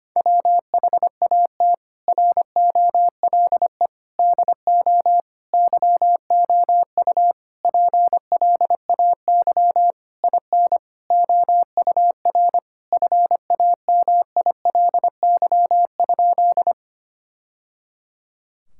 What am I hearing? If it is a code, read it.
Morse code, heard as a keyed tone, about 25 wpm, 700 Hz. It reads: WHAT ROLE DO YOU PLAY IN OUR FAMILY?
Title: Sentences from Top 1000 Words - Review Part 6 - 25wpm